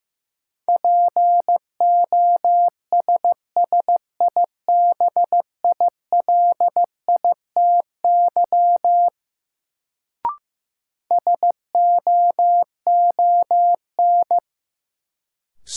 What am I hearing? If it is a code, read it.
Morse code, heard as a keyed tone, about 15 wpm, 700 Hz. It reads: POSSIBILITY SOON
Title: Sets of 3 Words - New Word Drill - 15wpm